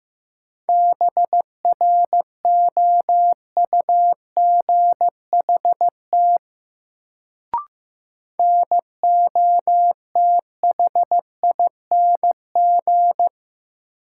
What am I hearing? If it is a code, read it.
Morse code, heard as a keyed tone, about 15 wpm, 700 Hz. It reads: BROUGHT NOTHING
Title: Sentences from Top 500 Words - New Word Drill - 15wpm